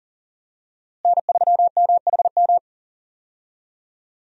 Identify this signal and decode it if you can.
Morse code, heard as a keyed tone, about 37 wpm, 700 Hz. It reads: N3MHM
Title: USA Call Signs - 5 Characters - Repeated 3x - 40wpm